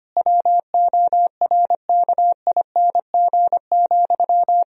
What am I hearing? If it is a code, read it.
Morse code, heard as a keyed tone, about 25 wpm, 700 Hz. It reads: WORKING,
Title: Sentences from Top 1000 Words - Review Part 4 - 25wpm